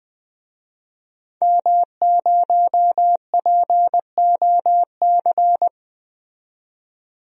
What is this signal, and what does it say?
Morse code, heard as a keyed tone, about 21 wpm, 700 Hz. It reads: M0POC
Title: DXCC-100 Call Signs - 5 Characters - 20wpm